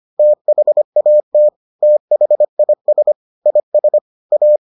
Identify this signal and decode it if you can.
Morse code, heard as a keyed tone, about 25 wpm, 600 Hz. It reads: THAT THIS IS A